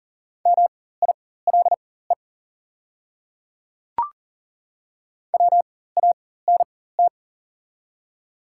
Morse code, heard as a keyed tone, about 40 wpm, spaced as for 19 wpm, 700 Hz. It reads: MILE WANT
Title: 4x Character Spacing - Top 500 Words - 40wpm